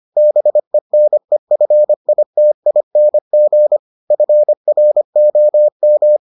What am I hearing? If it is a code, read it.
Morse code, heard as a keyed tone, about 25 wpm, 600 Hz. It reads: BENEFITING FROM